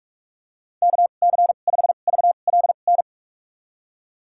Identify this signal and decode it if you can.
Morse code, heard as a keyed tone, about 46 wpm, spaced as for 33 wpm, 700 Hz. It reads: KC5VLN
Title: USA Call Signs - 2x Character Spacing - 45wpm